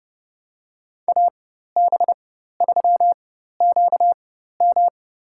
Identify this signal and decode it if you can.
Morse code, heard as a keyed tone, about 30 wpm, spaced as for 14 wpm, 700 Hz. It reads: AB3QM
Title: USA Call Signs - 5 Characters - 4x Character Spacing - 30wpm